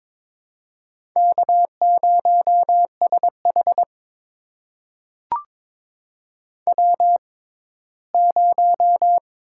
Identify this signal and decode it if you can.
Morse code, heard as a keyed tone, about 22 wpm, spaced as for 15 wpm, 700 Hz. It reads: K0SH W0
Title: USA Call Signs - 4 Characters - 6x Character Spacing - 22wpm